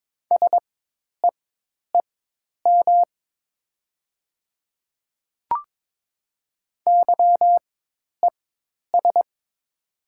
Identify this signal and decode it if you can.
Morse code, heard as a keyed tone, about 22 wpm, spaced as for 10 wpm, 700 Hz. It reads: SEEM YES
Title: 4x Character Spacing - Top 500 Words - 22wpm